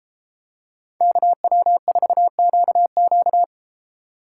Morse code, heard as a keyed tone, about 33 wpm, 700 Hz. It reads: KW4QQ